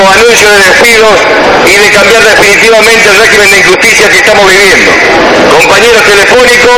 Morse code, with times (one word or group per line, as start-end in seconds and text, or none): none